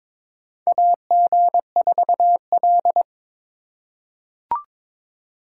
Morse code, heard as a keyed tone, about 23 wpm, 700 Hz. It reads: AG4L
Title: USA Call Signs - 4 Characters - 8x Character Spacing - 22wpm